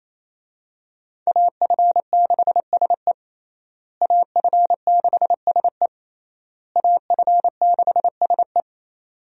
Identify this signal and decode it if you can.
Morse code, heard as a keyed tone, about 28 wpm, 700 Hz. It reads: AF6SE AF6SE AF6SE
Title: USA Call Signs - 5 Characters - Repeated 3x - 28wpm